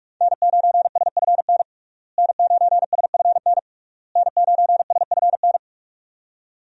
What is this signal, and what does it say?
Morse code, heard as a keyed tone, about 41 wpm, 700 Hz. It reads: N9SFN N9SFN N9SFN
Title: USA Call Signs - 5 Characters - Repeated 3x - 45wpm